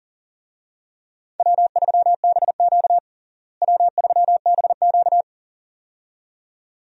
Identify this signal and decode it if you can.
Morse code, heard as a keyed tone, about 40 wpm, 700 Hz. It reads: W3BQ W3BQ
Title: USA Call Signs - 4 Characters - Repeated 2x - 40wpm